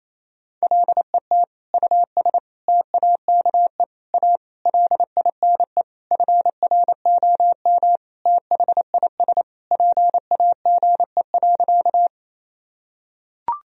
Morse code, heard as a keyed tone, about 28 wpm, 700 Hz. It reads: LET US TAKE A LINE FROM THIS PAGE.
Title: Sentences from Top 1000 Words - Review Part 1 - 28wpm